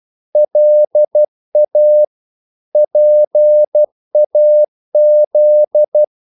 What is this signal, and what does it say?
Morse code, heard as a keyed tone, about 12 wpm, 600 Hz. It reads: LA PAZ
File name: CW / Practice / NationalCapitals12